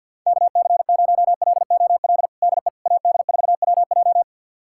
Morse code, heard as a keyed tone, about 52 wpm, 690 Hz. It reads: KC0LOL DE AD4WJ
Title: USA Callsign DE Callsign 50wpm